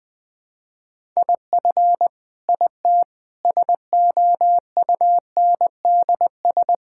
Morse code, heard as a keyed tone, about 20 wpm, 700 Hz. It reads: IF IT SOUNDS
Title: Sentences with 9-Word Phrases - 20wpm